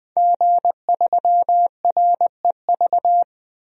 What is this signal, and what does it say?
Morse code, heard as a keyed tone, about 20 wpm, 700 Hz. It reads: G3REV
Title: DXCC-100 Call Signs - 5 Characters - Repeated 3x - 20wpm